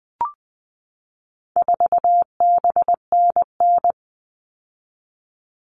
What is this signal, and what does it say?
Morse code, heard as a keyed tone, about 20 wpm, 700 Hz. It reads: 4BNN
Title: All Cut Numbers - 4-Digit 20wpm